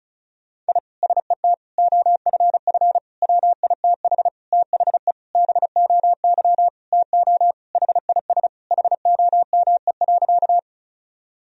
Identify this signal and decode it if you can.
Morse code, heard as a keyed tone, about 36 wpm, 700 Hz. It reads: I SET OFF WITH THE BOY TO HIS HOME.
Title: Sentences from Top 1000 Words - Review Part 1 - 35wpm